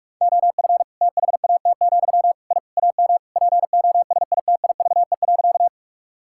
Morse code, heard as a keyed tone, about 46 wpm, 690 Hz. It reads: OF THAT, I AM POSITIVE.
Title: Sentences from Top 1000 Words - Review Part 5 - 45wpm